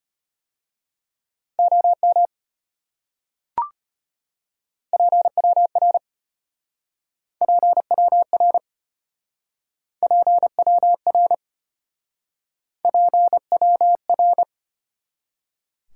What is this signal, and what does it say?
Morse code, heard as a keyed tone, about 33 wpm, 700 Hz. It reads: OM PWR PWR PWR PWR